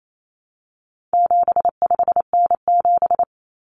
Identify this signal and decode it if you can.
Morse code, heard as a keyed tone, about 28 wpm, 700 Hz. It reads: 75N7
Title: Common Cut Numbers - 4-Digit 28wpm